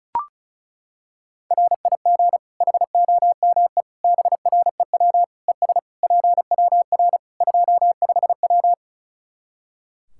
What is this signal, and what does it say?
Morse code, heard as a keyed tone, about 36 wpm, 700 Hz. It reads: RIG HOME BREW ES PWR 25W
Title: Practice QSO - Exchange 2 - Info - 35wpm